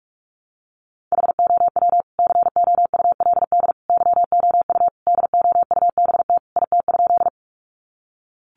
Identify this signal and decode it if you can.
Morse code, heard as a keyed tone, about 44 wpm, 700 Hz. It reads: HOW COULD YOU DOUBT IT?
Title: Sentences from Top 1500 Words 45wpm